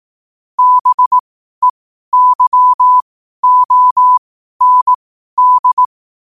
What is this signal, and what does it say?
Morse code, heard as a keyed tone, about 18 wpm, spaced as for 12 wpm, 1000 Hz. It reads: BEYOND
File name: qotdfiles / 12wpm